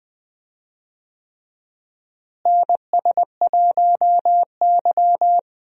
Morse code, heard as a keyed tone, about 20 wpm, 700 Hz. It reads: NS1Y
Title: USA Call Signs - 4 Characters - Repeated 3x - 20wpm